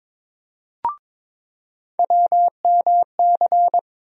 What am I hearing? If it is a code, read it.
Morse code, heard as a keyed tone, about 22 wpm, 700 Hz. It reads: WMC
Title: USA Call Sign Postfixes - 3 Letters 22wpm